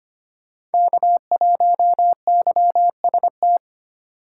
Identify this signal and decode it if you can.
Morse code, heard as a keyed tone, about 31 wpm, 700 Hz. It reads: K1YST